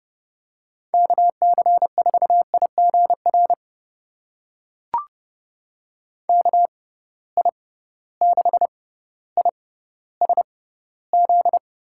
Morse code, heard as a keyed tone, about 32 wpm, spaced as for 16 wpm, 700 Hz. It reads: KC4IGR KI6ISZ
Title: USA Call Signs - 6 Characters - 6x Character Spacing - 30wpm